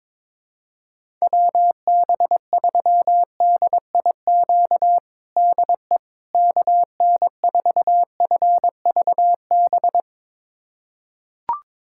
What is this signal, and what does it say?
Morse code, heard as a keyed tone, about 21 wpm, 700 Hz. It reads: WB3DIQ DE KN4FVB
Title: USA Callsign DE Callsign - 6 Characters - 22wpm